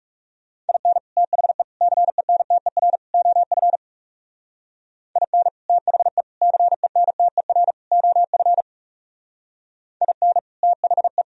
Morse code, heard as a keyed tone, about 39 wpm, 700 Hz. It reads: IN THE CENTER OF IN THE CENTER OF IN THE